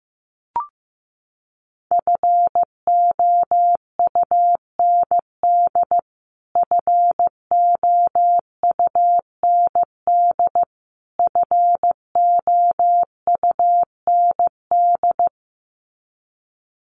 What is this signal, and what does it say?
Morse code, heard as a keyed tone, about 15 wpm, 700 Hz. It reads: FOUND FOUND FOUND
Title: Top 500 Words Repeated 15wpm